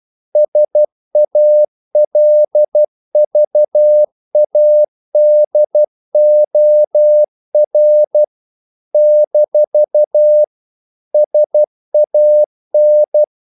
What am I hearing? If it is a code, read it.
Morse code, heard as a keyed tone, about 12 wpm, 600 Hz. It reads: SALVADOR - SAN
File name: CW / Practice / NationalCapitals12